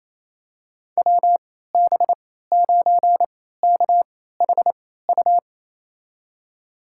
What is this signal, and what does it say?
Morse code, heard as a keyed tone, about 28 wpm, spaced as for 16 wpm, 700 Hz. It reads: WB9KHU
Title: USA Call Signs - 3x Character Spacing - 28wpm